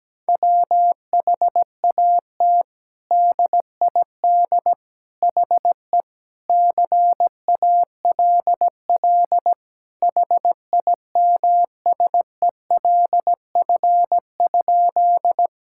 Morse code, heard as a keyed tone, about 17 wpm, 700 Hz. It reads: WHAT DID HE CALL HIMSELF?
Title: Sentences from Top 1600 Words 17wpm